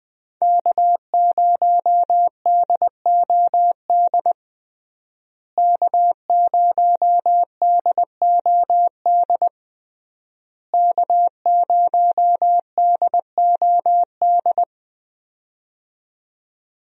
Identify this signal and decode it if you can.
Morse code, heard as a keyed tone, about 20 wpm, 700 Hz. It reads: K0DOD K0DOD K0DOD
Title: USA Call Signs - 5 Characters - Repeated 3x - 20wpm